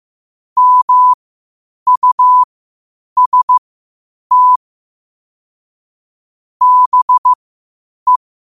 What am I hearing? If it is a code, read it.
Morse code, heard as a keyed tone, about 15 wpm, spaced as for 8 wpm, 1000 Hz. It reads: MUST BE
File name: qotdfiles / 8wpm